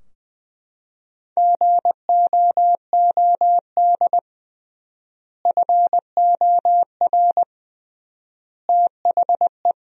Morse code, heard as a keyed tone, about 20 wpm, 700 Hz. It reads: GOOD FOR THE
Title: 3x Word-Spacing - Sentences from Top 300 Words - 20wpm